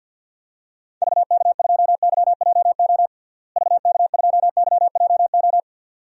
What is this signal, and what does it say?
Morse code, heard as a keyed tone, about 52 wpm, 700 Hz. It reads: VK2YJO VK2YJO
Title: DXCC-100 Call Signs - 6 Characters - Repeated 2x - 50wpm